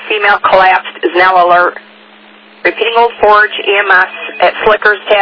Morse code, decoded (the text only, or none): none